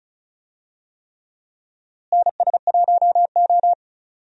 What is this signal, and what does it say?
Morse code, heard as a keyed tone, about 34 wpm, 700 Hz. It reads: NS1O
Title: USA Call Signs - 4 Characters - Repeated 3x - 35wpm